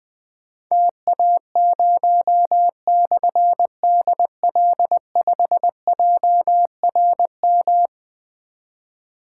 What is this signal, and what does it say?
Morse code, heard as a keyed tone, about 20 wpm, 700 Hz. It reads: TA0/DL5JRM
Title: Difficult Call Signs 20wpm